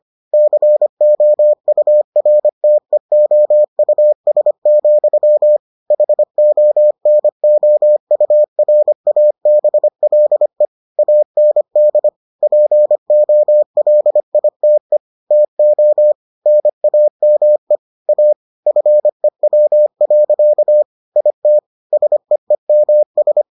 Morse code, heard as a keyed tone, about 25 wpm, 600 Hz. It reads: COURTEOUS, HONOURABLE AND POLITE TO NAME A FEW. IT SEEMS